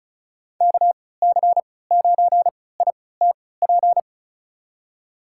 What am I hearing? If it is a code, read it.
Morse code, heard as a keyed tone, about 36 wpm, spaced as for 20 wpm, 700 Hz. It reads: KC9ITP
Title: USA Call Signs - 6 Characters - 3x Character Spacing - 35wpm